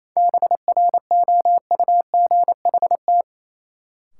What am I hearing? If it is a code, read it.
Morse code, heard as a keyed tone, about 28 wpm, 700 Hz. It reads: BROUGHT
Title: Sentences from Top 500 Words - New Word Drill - 28wpm